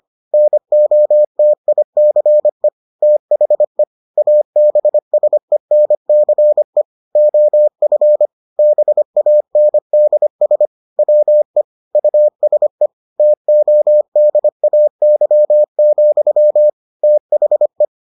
Morse code, heard as a keyed tone, about 25 wpm, 600 Hz. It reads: NOTICE THE ABSENCE OF BANDS WE USE TODAY, THE